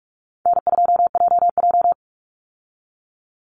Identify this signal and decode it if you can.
Morse code, heard as a keyed tone, about 44 wpm, 700 Hz. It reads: N2JJ
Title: Easy Call Signs Repeated 2x - 45wpm